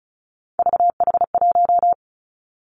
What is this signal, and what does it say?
Morse code, heard as a keyed tone, about 36 wpm, 700 Hz. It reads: VH1
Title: Three Challenging Letter-Number Combinations 35wpm